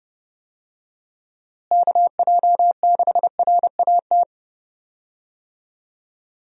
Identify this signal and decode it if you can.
Morse code, heard as a keyed tone, about 32 wpm, 700 Hz. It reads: KJ6RAT
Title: USA Call Signs - 6 Characters - Repeated 3x - 30wpm